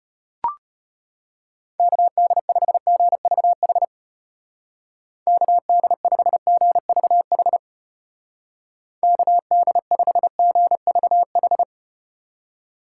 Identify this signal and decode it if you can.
Morse code, heard as a keyed tone, about 34 wpm, 700 Hz. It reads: KD5GVH KD5GVH KD5GVH